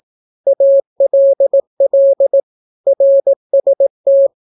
Morse code, heard as a keyed tone, about 17 wpm, 550 Hz. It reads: ALL RST
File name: naukapodstaw / skroty / skroty